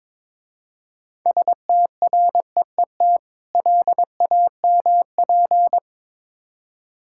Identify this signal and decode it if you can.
Morse code, heard as a keyed tone, about 24 wpm, 700 Hz. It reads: STREET LAMP
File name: Easy Words - Speed-Racing with Hints - 22wpm